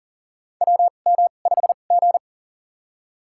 Morse code, heard as a keyed tone, about 39 wpm, spaced as for 28 wpm, 700 Hz. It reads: WM5G
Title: USA Call Signs - 2x Character Spacing - 40wpm